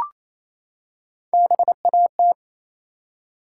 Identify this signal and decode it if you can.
Morse code, heard as a keyed tone, about 28 wpm, 700 Hz. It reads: BAT
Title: Three Letter Words - 28wpm